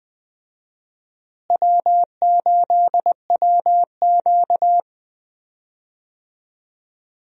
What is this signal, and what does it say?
Morse code, heard as a keyed tone, about 20 wpm, 700 Hz. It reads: W8WQ
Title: USA Call Signs - 4 Characters - Repeated 2x - 20wpm